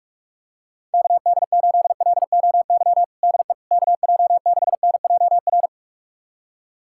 Morse code, heard as a keyed tone, about 46 wpm, 700 Hz. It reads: KD8LOY DE KJ6NJR
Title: USA Callsign DE Callsign - 6 Characters - 45wpm